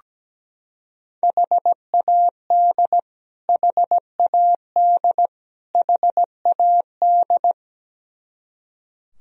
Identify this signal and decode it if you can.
Morse code, heard as a keyed tone, about 17 wpm, 700 Hz. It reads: HAD HAD HAD
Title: Top 100 Words Repeated 17wpm